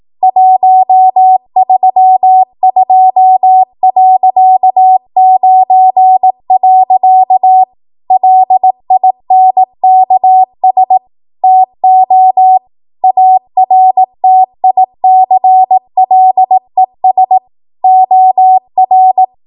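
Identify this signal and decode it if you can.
Morse code, heard as a keyed tone, about 18 wpm, 750 Hz. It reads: 132.9. LINKS TO ARTICLES OR
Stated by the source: ARRL 18 wpm Morse code practice archive transmission.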